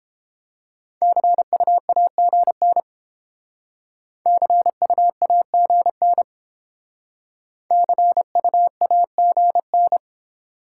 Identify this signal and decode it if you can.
Morse code, heard as a keyed tone, about 32 wpm, 700 Hz. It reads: CUAGN CUAGN CUAGN